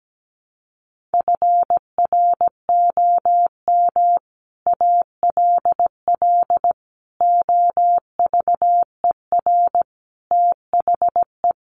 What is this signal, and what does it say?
Morse code, heard as a keyed tone, about 17 wpm, 700 Hz. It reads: FROM ALL OVER THE
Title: Sets of 5 Words Encore 17wpm